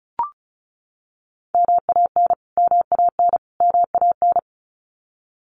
Morse code, heard as a keyed tone, about 34 wpm, 700 Hz. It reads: MAN MAN MAN
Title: Top 500 Words Repeated 35wpm